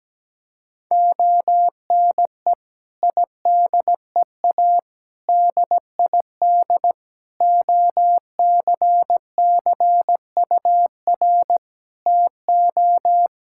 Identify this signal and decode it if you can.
Morse code, heard as a keyed tone, about 17 wpm, 700 Hz. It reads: ONE IDEA DID OCCUR TO
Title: Sentences from Top 1000 Words - Review Part 6 - 17wpm